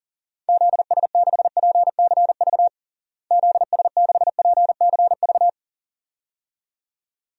Morse code, heard as a keyed tone, about 40 wpm, 700 Hz. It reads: ZS6PCV ZS6PCV
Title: DXCC-100 Call Signs - 6 Characters - Repeated 2x - 40wpm